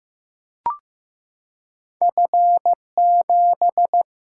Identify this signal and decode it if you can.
Morse code, heard as a keyed tone, about 15 wpm, 700 Hz. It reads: F 7
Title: Single-Letter Single-Number 15wpm